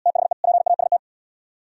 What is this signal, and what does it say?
Morse code, heard as a keyed tone, about 76 wpm, 700 Hz. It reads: THE PAST